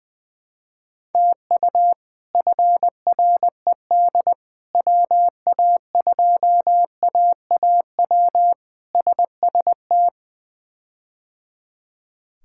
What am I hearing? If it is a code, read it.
Morse code, heard as a keyed tone, about 20 wpm, 700 Hz. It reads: TU FRED WA2AAW SST
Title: K1USN SST QSO Practice - 20wpm